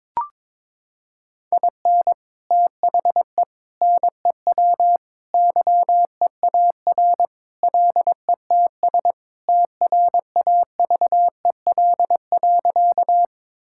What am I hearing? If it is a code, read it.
Morse code, heard as a keyed tone, about 22 wpm, 700 Hz. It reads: IN THE NEW YEAR LETS TRAVEL.
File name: practice / 22wpm / Sentences with 3-Word Phrases - 22wpm